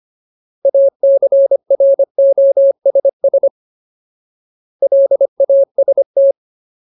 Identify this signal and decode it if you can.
Morse code, heard as a keyed tone, about 25 wpm, 550 Hz. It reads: ACROSS LAST
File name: Ultimate Word List / 25WPM Ultimate